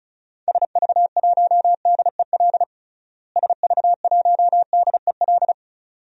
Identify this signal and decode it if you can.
Morse code, heard as a keyed tone, about 36 wpm, 700 Hz. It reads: SV1DEL SV1DEL
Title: DXCC-100 Call Signs - 6 Characters - Repeated 2x - 35wpm